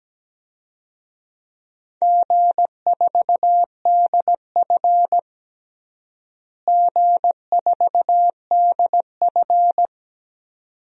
Morse code, heard as a keyed tone, about 17 wpm, 700 Hz. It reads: G4DF G4DF
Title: DXCC-100 Call Signs - 4 Characters - Repeated 2x - 17wpm